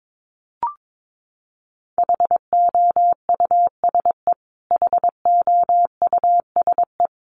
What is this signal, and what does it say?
Morse code, heard as a keyed tone, about 22 wpm, 700 Hz. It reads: HOUSE HOUSE
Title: Top 500 Words Repeated 22wpm